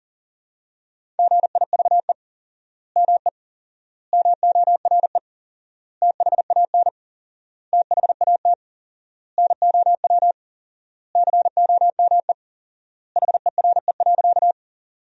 Morse code, heard as a keyed tone, about 40 wpm, 700 Hz. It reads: GIVE ME MORE THAN THAT NOW COME HERE.
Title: 4x Word-Spacing - Sentences from Top 200 Words - 40wpm